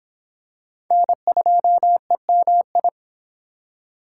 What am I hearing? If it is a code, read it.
Morse code, heard as a keyed tone, about 26 wpm, 700 Hz. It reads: N2EMI